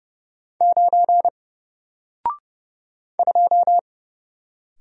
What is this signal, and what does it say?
Morse code, heard as a keyed tone, about 30 wpm, 700 Hz. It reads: 9 2
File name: Single-Letter-Number-30wpm